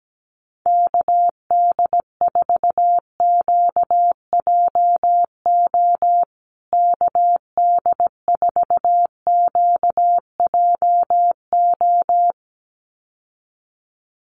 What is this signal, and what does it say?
Morse code, heard as a keyed tone, about 17 wpm, 700 Hz. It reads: KD4QJO KD4QJO
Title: USA Call Signs Repeated 2x 17wpm